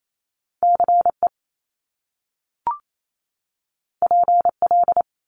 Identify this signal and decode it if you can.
Morse code, heard as a keyed tone, about 28 wpm, 700 Hz. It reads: CE PL
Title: Top Two Letter Combinations in Words 28wpm